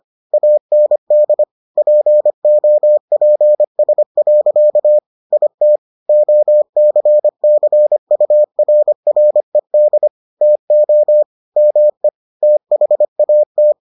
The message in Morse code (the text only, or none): AND POPS. IT OCCURRED TO ME THAT